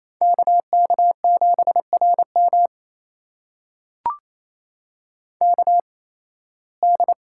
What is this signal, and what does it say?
Morse code, heard as a keyed tone, about 28 wpm, spaced as for 18 wpm, 700 Hz. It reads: KK7RM KD
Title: USA Call Signs - 8x Character Spacing - 28wpm